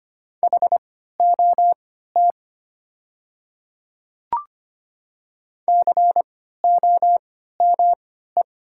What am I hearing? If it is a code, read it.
Morse code, heard as a keyed tone, about 25 wpm, spaced as for 14 wpm, 700 Hz. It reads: HOT COME
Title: 3x Character Spacing - Top 100 Words - 25wpm